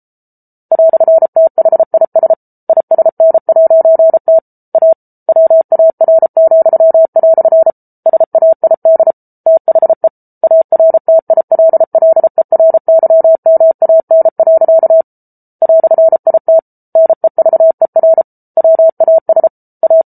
Morse code, heard as a keyed tone, about 34 wpm, 650 Hz. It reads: "THIS ISN'T A WAR," SAID THE ARTILLERYMAN. "IT NEVER WAS A